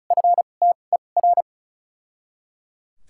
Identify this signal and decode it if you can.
Morse code, heard as a keyed tone, about 34 wpm, spaced as for 25 wpm, 700 Hz. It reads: FTER
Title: 2x Character Spacing - Top 500 Words - 35wpm